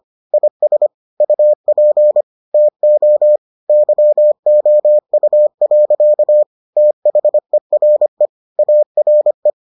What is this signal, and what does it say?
Morse code, heard as a keyed tone, about 25 wpm, 600 Hz. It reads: IS UP TO YOU. THERE ARE